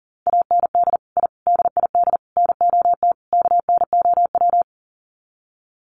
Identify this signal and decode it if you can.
Morse code, heard as a keyed tone, about 37 wpm, 700 Hz. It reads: AND I DID NOT KNOW
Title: Sets of 5 Words Encore 40wpm